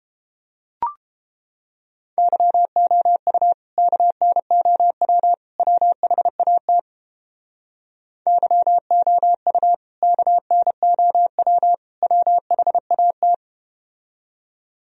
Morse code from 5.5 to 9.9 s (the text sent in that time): WHAT YOU